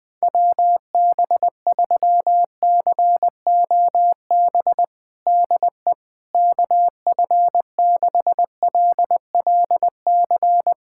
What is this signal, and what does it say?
Morse code, heard as a keyed tone, about 20 wpm, 700 Hz. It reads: WB3COB DE KF6LLC
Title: USA Callsign DE Callsign - 6 Characters - 20wpm